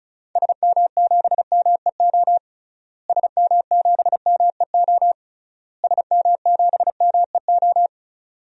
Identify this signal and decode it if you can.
Morse code, heard as a keyed tone, about 36 wpm, 700 Hz. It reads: SM7MEO SM7MEO SM7MEO
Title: DXCC-100 Call Signs - 6 Characters - Repeated 3x - 35wpm